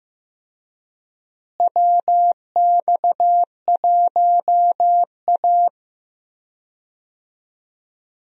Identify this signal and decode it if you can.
Morse code, heard as a keyed tone, about 15 wpm, 700 Hz. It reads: WX1A
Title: USA Call Signs - 4 Characters - Repeated 2x - 15wpm